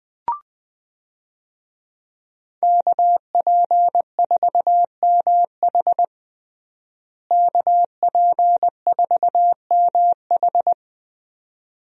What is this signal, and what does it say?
Morse code, heard as a keyed tone, about 20 wpm, 700 Hz. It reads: KP4MH KP4MH
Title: DXCC-100 Call Signs - 5 Characters - Repeated 2x - 20wpm